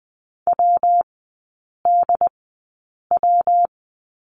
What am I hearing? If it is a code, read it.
Morse code, heard as a keyed tone, about 20 wpm, spaced as for 8 wpm, 700 Hz. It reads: WDW
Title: 114 Lesson 19e - Letter w - Three Characters 20wpm